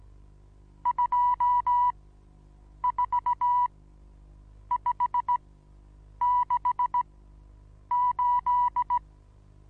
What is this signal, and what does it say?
Morse code, heard as a keyed tone, about 17 wpm, spaced as for 7 wpm, 980 Hz. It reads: 24568
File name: Onlinelehrgang / morsen / DJ4UF Morsen25